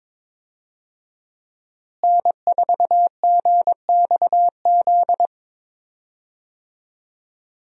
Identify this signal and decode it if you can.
Morse code, heard as a keyed tone, about 22 wpm, 700 Hz. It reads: N4GXZ
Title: USA Call Signs - 5 Characters - Repeated 3x - 22wpm